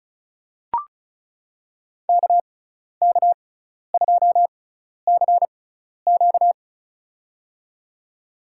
Morse code, heard as a keyed tone, about 34 wpm, spaced as for 12 wpm, 700 Hz. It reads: KK2CQ
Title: USA Call Signs - 5 Characters - 6x Character Spacing - 35wpm